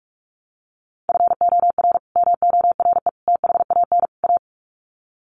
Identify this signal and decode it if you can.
Morse code, heard as a keyed tone, about 44 wpm, 700 Hz. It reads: FOR MORE THAN A
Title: Sets of 4 Words 45wpm